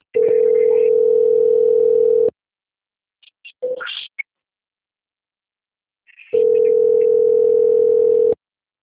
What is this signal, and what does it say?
Ringback tone (North America), 440 + 480 Hz.
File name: out- - - - - .67057